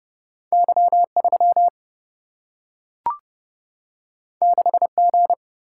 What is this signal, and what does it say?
Morse code, heard as a keyed tone, about 30 wpm, 700 Hz. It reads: Y3 6G
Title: Single-Letter Single-Number 30wpm